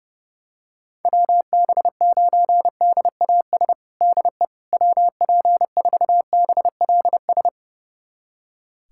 Morse code, heard as a keyed tone, about 31 wpm, 700 Hz. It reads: WB9DAS DE WP4BLS
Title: USA Callsign DE Callsign - 6 Characters - 30wpm